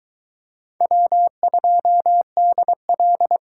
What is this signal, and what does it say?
Morse code, heard as a keyed tone, about 22 wpm, 700 Hz. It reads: W2DL